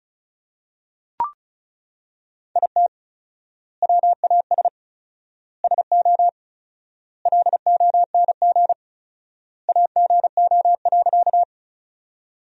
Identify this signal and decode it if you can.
Morse code, heard as a keyed tone, about 34 wpm, 700 Hz. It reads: IT WAS SO LONG AGO.
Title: 4x Word-Spacing - Sentences from Top 500 Words - 35wpm